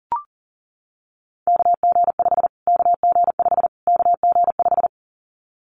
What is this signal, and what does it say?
Morse code, heard as a keyed tone, about 40 wpm, 700 Hz. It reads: KG5 KG5 KG5
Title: Callsign Prefixes for Top 25 DXCC - Repeated 3x - 40wpm